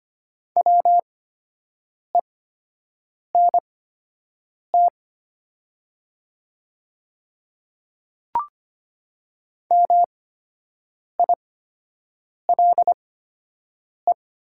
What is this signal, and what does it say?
Morse code, heard as a keyed tone, about 25 wpm, spaced as for 7 wpm, 700 Hz. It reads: WENT MILE
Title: 8x Character Spacing - Top 500 Words - 25wpm